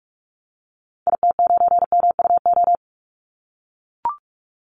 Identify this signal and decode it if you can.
Morse code, heard as a keyed tone, about 44 wpm, 700 Hz. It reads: IT9MUO
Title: Call Signs - 45wpm